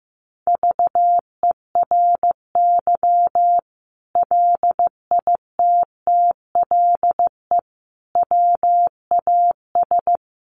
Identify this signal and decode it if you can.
Morse code, heard as a keyed tone, about 15 wpm, 700 Hz. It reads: VERY LITTLE WAS
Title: Sentences from Top 600 Words 15wpm